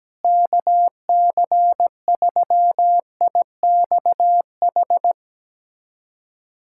Morse code, heard as a keyed tone, about 17 wpm, 700 Hz. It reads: KC3IXH
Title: USA Callsign DE Callsign 17wpm